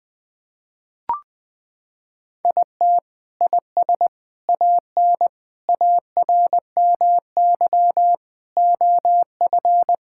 Morse code, heard as a keyed tone, about 21 wpm, 700 Hz. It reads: IT IS AN ARMY OF
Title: Sentences from Top 1000 Words - Review Part 5 - 20wpm